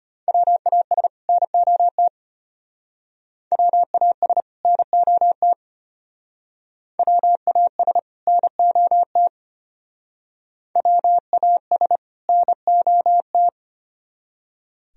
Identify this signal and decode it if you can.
Morse code, heard as a keyed tone, about 32 wpm, 700 Hz. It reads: WAS NOT WAS NOT WAS NOT WAS NOT